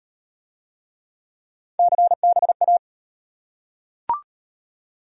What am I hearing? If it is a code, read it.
Morse code, heard as a keyed tone, about 33 wpm, 700 Hz. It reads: CBA